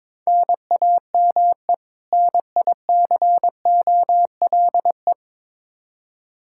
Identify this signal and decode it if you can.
Morse code, heard as a keyed tone, about 22 wpm, 700 Hz. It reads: NAME NICOLE
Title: Practice QSO - Exchange 1 - Beginning - 22wpm